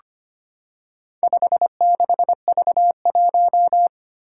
Morse code, heard as a keyed tone, about 25 wpm, 700 Hz. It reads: 56V1
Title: Four Challenging Letter-Number Combinations 25wpm